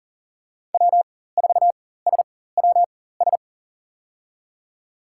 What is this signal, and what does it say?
Morse code, heard as a keyed tone, about 40 wpm, spaced as for 19 wpm, 700 Hz. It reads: W4SWS
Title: USA Call Signs - 5 Characters - 4x Character Spacing - 40wpm